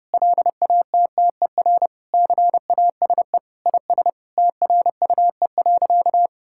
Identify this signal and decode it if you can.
Morse code, heard as a keyed tone, about 32 wpm, 700 Hz. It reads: LATTER CASE IS TRUE.
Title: Sentences from Top 1600 Words 30wpm